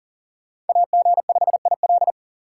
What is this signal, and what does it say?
Morse code, heard as a keyed tone, about 53 wpm, 700 Hz. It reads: AG5IL